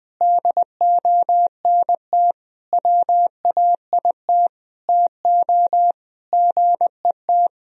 Morse code, heard as a keyed tone, about 21 wpm, 700 Hz. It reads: DONT WAIT TO GET
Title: Sentences from Top 1000 Words - Review Part 2 - 20wpm